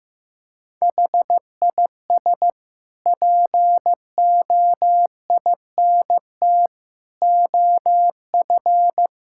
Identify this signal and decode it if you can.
Morse code, heard as a keyed tone, about 15 wpm, 700 Hz. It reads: HIS POINT OF
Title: Sentences from Top 1000 Words - Review Part 6 - 15wpm